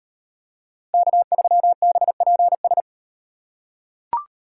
Morse code, heard as a keyed tone, about 35 wpm, 700 Hz. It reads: K3BPS